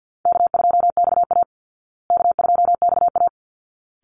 Morse code, heard as a keyed tone, about 50 wpm, 700 Hz. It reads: K2XA K2XA
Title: Easy Call Signs Repeated 2x - 50wpm